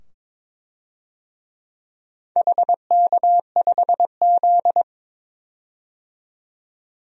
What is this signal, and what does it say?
Morse code, heard as a keyed tone, about 22 wpm, 700 Hz. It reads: HK5Z
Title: DXCC-100 Call Signs - 4 Characters - 22wpm